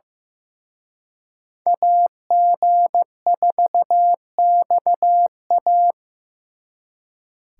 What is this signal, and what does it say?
Morse code, heard as a keyed tone, about 18 wpm, 700 Hz. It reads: AG4XA
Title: Speed Racing - USA Call Signs 15wpm